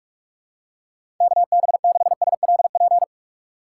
Morse code, heard as a keyed tone, about 43 wpm, 700 Hz. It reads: KB6SLP
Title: USA Call Signs - 6 Characters - 6x Character Spacing - 45wpm